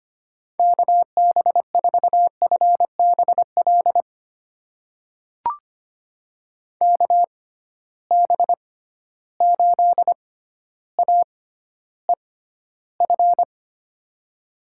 Morse code, heard as a keyed tone, about 25 wpm, spaced as for 13 wpm, 700 Hz. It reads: KB4FBL KB8AEF
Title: USA Call Signs - 6 Characters - 6x Character Spacing - 25wpm